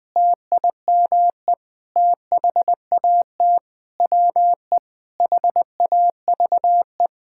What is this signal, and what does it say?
Morse code, heard as a keyed tone, about 19 wpm, 700 Hz. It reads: TIME THAT WE HAVE
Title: Sentences with 7-Word Phrases - 20wpm